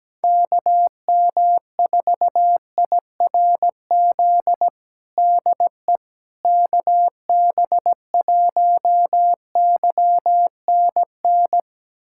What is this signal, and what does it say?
Morse code, heard as a keyed tone, about 17 wpm, 700 Hz. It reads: KM4IRZ DE KB1YNN
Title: USA Callsign DE Callsign - 6 Characters - 17wpm